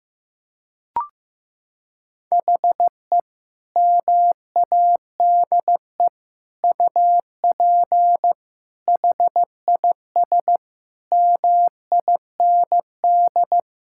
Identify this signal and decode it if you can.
Morse code, heard as a keyed tone, about 15 wpm, 700 Hz. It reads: HE MADE UP HIS MIND
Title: Sentences from Top 1600 Words 15wpm